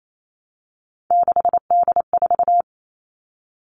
Morse code, heard as a keyed tone, about 28 wpm, 700 Hz. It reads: 6D4
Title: Three Challenging Letter-Number Combinations 28wpm